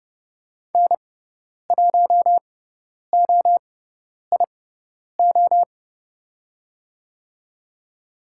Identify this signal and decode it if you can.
Morse code, heard as a keyed tone, about 30 wpm, spaced as for 10 wpm, 700 Hz. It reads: N1OIO
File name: Lesson 16g - Letter c - Call Signs 30wpm